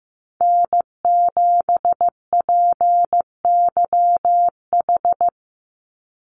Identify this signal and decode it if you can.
Morse code, heard as a keyed tone, about 15 wpm, 700 Hz. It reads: N7PYH
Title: USA Call Signs 15wpm